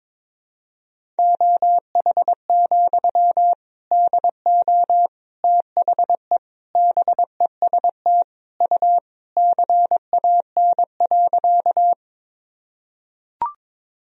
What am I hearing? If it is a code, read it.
Morse code, heard as a keyed tone, about 22 wpm, 700 Hz. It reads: OH, DO THE BEST U CAN.
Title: Sentences from Top 500 Words and Abbreviations 22wpm